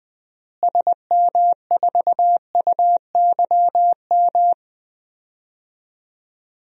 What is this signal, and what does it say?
Morse code, heard as a keyed tone, about 20 wpm, 700 Hz. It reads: SM4UYM
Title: DXCC-100 Call Signs - 6 Characters - Repeated 2x - 20wpm